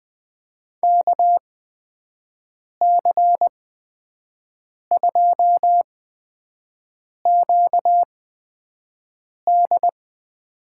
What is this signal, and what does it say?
Morse code, heard as a keyed tone, about 21 wpm, spaced as for 5 wpm, 700 Hz. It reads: KC2QD
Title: USA Call Signs - 8x Character Spacing - 20wpm